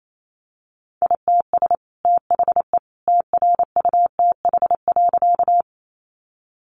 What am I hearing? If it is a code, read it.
Morse code, heard as a keyed tone, about 28 wpm, 700 Hz. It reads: ITS THE TRUTH.
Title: Sentences from Top 800 Words 28wpm